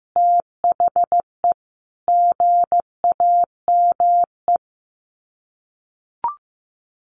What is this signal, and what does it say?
Morse code, heard as a keyed tone, about 15 wpm, 700 Hz. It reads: THE GAME
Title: Sets of 5 Words Encore2 15wpm